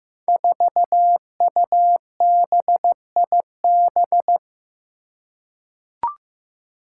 Morse code, heard as a keyed tone, about 15 wpm, 700 Hz. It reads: 4UBIB
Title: Five Challenging Letter-Number Combinations 15wpm